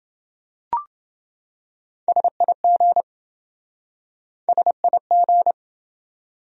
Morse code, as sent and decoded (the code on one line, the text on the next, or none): ... .. --. / ... .. --.
SIG SIG